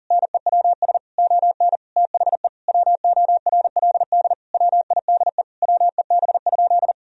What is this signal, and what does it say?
Morse code, heard as a keyed tone, about 37 wpm, 700 Hz. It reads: NEWS ON THE WORLD WIDE WEB?
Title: Sentences from Top 1000 Words - Review Part 4 - 40wpm